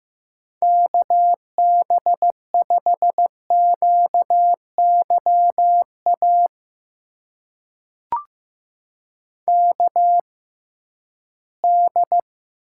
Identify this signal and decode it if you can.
Morse code, heard as a keyed tone, about 15 wpm, spaced as for 11 wpm, 700 Hz. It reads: KB5QYA KD
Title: USA Call Signs - 6 Characters - 6x Character Spacing - 15wpm